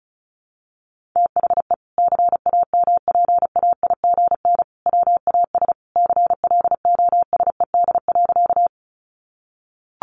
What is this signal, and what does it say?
Morse code, heard as a keyed tone, about 34 wpm, 700 Hz. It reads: THE CAMPAIGN WAS CLOSED.
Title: Sentences from Top 1400 Words 35wpm